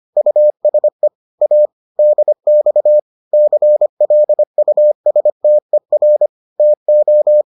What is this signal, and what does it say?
Morse code, heard as a keyed tone, about 25 wpm, 600 Hz. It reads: USE A DX CLUSTER TO